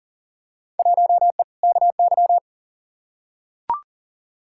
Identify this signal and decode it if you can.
Morse code, heard as a keyed tone, about 40 wpm, 700 Hz. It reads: 1E KY
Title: Field Day Exchanges - 40wpm